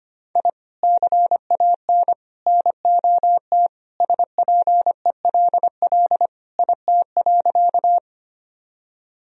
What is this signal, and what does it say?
Morse code, heard as a keyed tone, about 25 wpm, 700 Hz. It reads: I CAN NOT SPELL IT.
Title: Sentences from Top 1000 Words - Review Part 1 - 25wpm